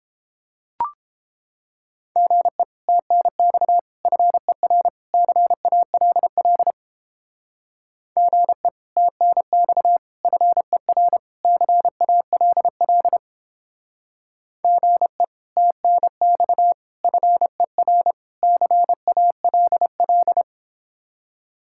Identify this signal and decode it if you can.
Morse code, heard as a keyed tone, about 29 wpm, 700 Hz. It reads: GE TNX FER CALL GE TNX FER CALL GE TNX FER CALL